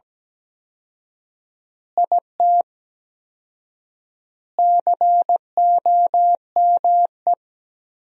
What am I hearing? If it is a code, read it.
Morse code, heard as a keyed tone, about 17 wpm, 700 Hz. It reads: IT COME
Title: 4x Word-Spacing - Sentences from Top 100 Words - 17wpm